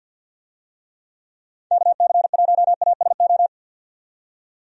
Morse code, heard as a keyed tone, about 51 wpm, 700 Hz. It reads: KC1ASO
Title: USA Call Signs - 6 Characters - 50wpm